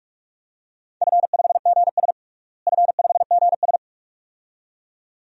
Morse code, heard as a keyed tone, about 46 wpm, 700 Hz. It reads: F5GS F5GS
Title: DXCC-100 Call Signs - 4 Characters - Repeated 2x - 45wpm